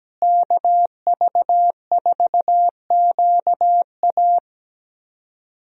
Morse code, heard as a keyed tone, about 17 wpm, 700 Hz. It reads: KV4QA
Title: USA Call Signs - 5 Characters - Repeated 3x - 17wpm